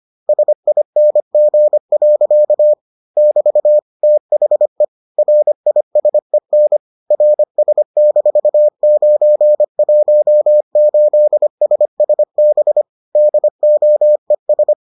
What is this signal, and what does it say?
Morse code, heard as a keyed tone, about 25 wpm, 600 Hz. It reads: SING. <BT> THE RISEN RS-918SSB DOES